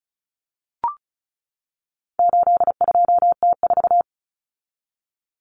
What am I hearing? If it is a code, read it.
Morse code, heard as a keyed tone, about 34 wpm, 700 Hz. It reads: 82T4
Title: Common Cut Numbers - 4-Digit 35wpm